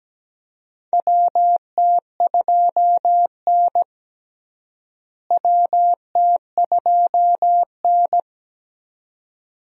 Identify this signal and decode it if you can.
Morse code, heard as a keyed tone, about 17 wpm, 700 Hz. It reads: WT2N WT2N
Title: USA Call Signs - 4 Characters - Repeated 2x - 17wpm